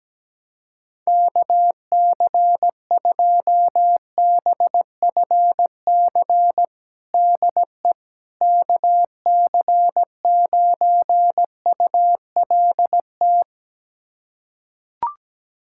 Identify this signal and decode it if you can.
Morse code, heard as a keyed tone, about 17 wpm, 700 Hz. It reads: KC2BFC DE KC9ULT
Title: USA Callsign DE Callsign - 6 Characters - 17wpm